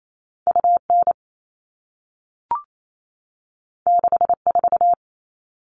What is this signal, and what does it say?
Morse code, heard as a keyed tone, about 28 wpm, 700 Hz. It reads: UN 64
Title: All Cut Numbers - 2-Digit 28wpm